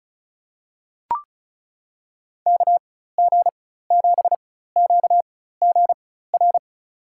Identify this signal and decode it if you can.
Morse code, heard as a keyed tone, about 34 wpm, spaced as for 16 wpm, 700 Hz. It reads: KG7QGR
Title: USA Call Signs - 6 Characters - 4x Character Spacing - 35wpm